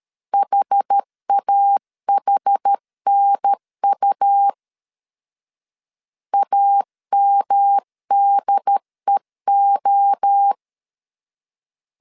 Morse code, heard as a keyed tone, about 13 wpm, 800 Hz. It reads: HAHNU AMDEO
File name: CW / DET / 1 3 1500C S A 12wpm 2905